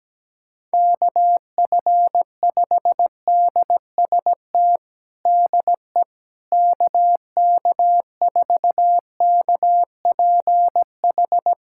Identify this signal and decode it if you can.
Morse code, heard as a keyed tone, about 17 wpm, 700 Hz. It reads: KF5DST DE KK4KPH
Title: USA Callsign DE Callsign 17wpm